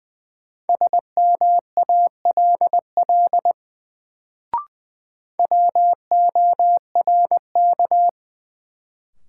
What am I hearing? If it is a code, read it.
Morse code, heard as a keyed tone, about 20 wpm, 700 Hz. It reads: SMALL WORK
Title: Sets of 2 Words - New Word Drill - 20wpm